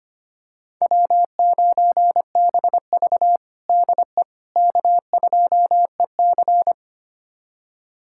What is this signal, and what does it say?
Morse code, heard as a keyed tone, about 25 wpm, 700 Hz. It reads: W9BV DE K2EC
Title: USA Callsign DE Callsign - 4 Characters - 25wpm